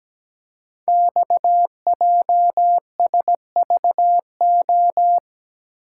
Morse code, heard as a keyed tone, about 17 wpm, 700 Hz. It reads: XJSVO
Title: Code Groups - 5 Letters - 17wpm